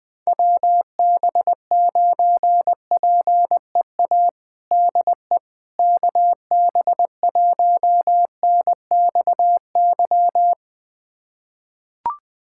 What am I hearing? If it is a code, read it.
Morse code, heard as a keyed tone, about 20 wpm, 700 Hz. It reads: WB9PEA DE KB1NXY
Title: USA Callsign DE Callsign - 6 Characters - 20wpm